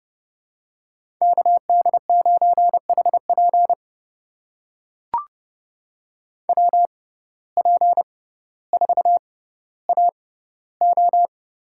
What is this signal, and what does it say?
Morse code, heard as a keyed tone, about 31 wpm, spaced as for 15 wpm, 700 Hz. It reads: KD9HP WP4AO
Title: USA Call Signs - 5 Characters - 6x Character Spacing - 30wpm